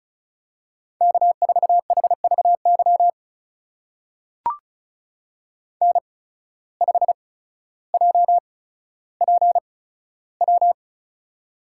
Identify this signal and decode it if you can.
Morse code, heard as a keyed tone, about 36 wpm, spaced as for 15 wpm, 700 Hz. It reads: K4HVY N5JPW
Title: USA Call Signs - 5 Characters - 8x Character Spacing - 35wpm